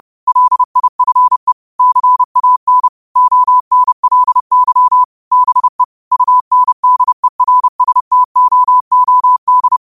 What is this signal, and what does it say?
Morse code, heard as a keyed tone, about 30 wpm, 1000 Hz. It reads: LIFE CAN ONLY BE UNDERSTOOD
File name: qotdfiles / 30wpm